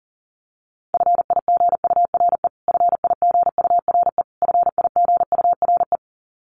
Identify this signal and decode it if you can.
Morse code, heard as a keyed tone, about 40 wpm, 700 Hz. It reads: FIGURE FIGURE FIGURE
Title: Top 500 Words Repeated 40wpm